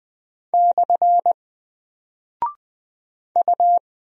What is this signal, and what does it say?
Morse code, heard as a keyed tone, about 20 wpm, 700 Hz. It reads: / U
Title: Single Character (Letter:Number:Punctuation:Prosign) Alt 20wpm